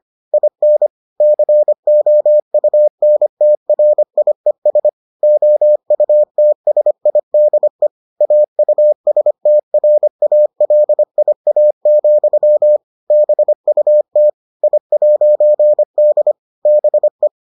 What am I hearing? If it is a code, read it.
Morse code, heard as a keyed tone, about 25 wpm, 600 Hz. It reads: IN COUNTRIES OUTSIDE AUSTRALIA, BUT I'D BE